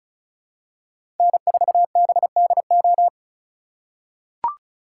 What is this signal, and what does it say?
Morse code, heard as a keyed tone, about 36 wpm, 700 Hz. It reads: N4BDO
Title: USA Call Signs - 3x Character Spacing - 35wpm